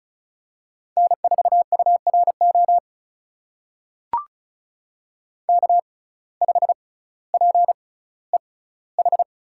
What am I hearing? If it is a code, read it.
Morse code, heard as a keyed tone, about 36 wpm, spaced as for 18 wpm, 700 Hz. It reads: N4URO K5PEH
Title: USA Call Signs - 5 Characters - 6x Character Spacing - 35wpm